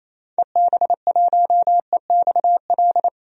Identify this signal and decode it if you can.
Morse code, heard as a keyed tone, about 27 wpm, 700 Hz. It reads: EB1EXL
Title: DXCC-100 Call Signs - 6 Characters - 28wpm